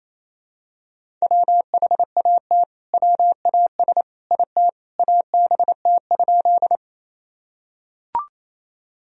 Morse code, heard as a keyed tone, about 28 wpm, 700 Hz. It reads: WHAT WAS IT ABT?
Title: Sentences from Top 100 Words and Abbreviations 28wpm